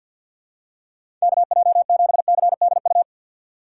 Morse code, heard as a keyed tone, about 53 wpm, 700 Hz. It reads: KJ7CDU
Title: USA Call Signs - 6 Characters - 6x Character Spacing - 50wpm